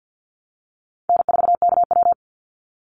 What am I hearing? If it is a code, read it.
Morse code, heard as a keyed tone, about 50 wpm, 700 Hz. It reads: N4KW
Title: Easy Call Signs - 50wpm